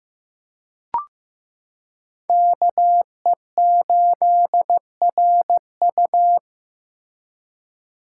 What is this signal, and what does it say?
Morse code, heard as a keyed tone, about 15 wpm, 700 Hz. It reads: KE8RU
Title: K1USN SST QSO Practice - 15wpm